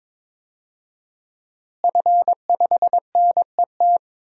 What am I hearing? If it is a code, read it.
Morse code, heard as a keyed tone, about 21 wpm, 700 Hz. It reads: F5NET
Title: DXCC-100 Call Signs - 5 Characters - Repeated 3x - 22wpm